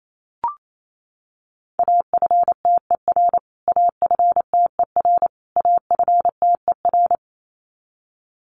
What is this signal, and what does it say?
Morse code, heard as a keyed tone, about 28 wpm, 700 Hz. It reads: AFTER AFTER AFTER
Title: Top 300 Words Repeated 28wpm